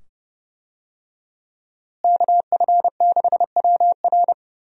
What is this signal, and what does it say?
Morse code, heard as a keyed tone, about 30 wpm, 700 Hz. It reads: KF6WR
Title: USA Call Signs - 5 Characters - Repeated 2x - 30wpm